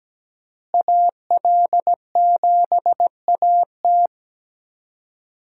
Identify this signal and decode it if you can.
Morse code, heard as a keyed tone, about 17 wpm, 700 Hz. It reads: AL7AT
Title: USA Call Signs - 5 Characters - Repeated 3x - 17wpm